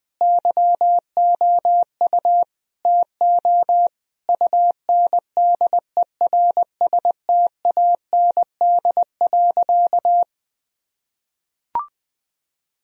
Morse code, heard as a keyed tone, about 19 wpm, 700 Hz. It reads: YOU TO UNDERSTAND.
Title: Sentences from Top 1000 Words - Review Part 3 - 20wpm